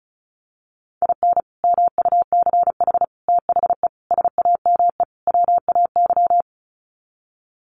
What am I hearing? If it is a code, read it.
Morse code, heard as a keyed tone, about 34 wpm, 700 Hz. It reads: IN MUCH THE SAME WAY
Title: Sets of 5 Words Encore 35wpm